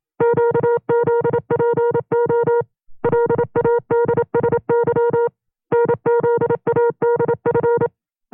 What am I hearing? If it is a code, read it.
Morse code, heard as a keyed tone, about 27 wpm, 470 Hz. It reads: QZPO LADSY NZADF
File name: FreeFiles / CW-practice / 25WPM-side1